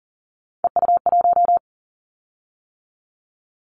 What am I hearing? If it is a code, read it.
Morse code, heard as a keyed tone, about 40 wpm, 700 Hz. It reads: EU1
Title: Callsign Prefixes for Top 100 DXCC - 40wpm